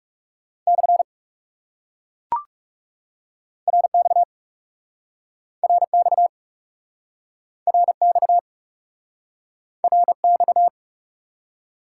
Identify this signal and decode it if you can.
Morse code, heard as a keyed tone, about 38 wpm, 700 Hz. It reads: / RX RX RX RX